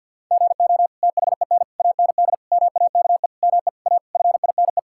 Morse code, heard as a keyed tone, about 50 wpm, 700 Hz. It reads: GO THEN AND MAKE ME A FINE
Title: Sentences from Top 1000 Words - Review Part 5 - 50wpm